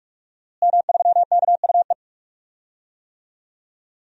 Morse code, heard as a keyed tone, about 46 wpm, 700 Hz. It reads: M3KUE
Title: DXCC-100 Call Signs - 5 Characters - Repeated 2x - 45wpm